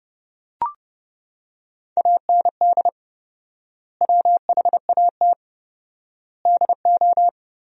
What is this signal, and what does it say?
Morse code, heard as a keyed tone, about 30 wpm, 700 Hz. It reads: AND WHAT DO
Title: 4x Word-Spacing - Sentences from Top 100 Words - 30wpm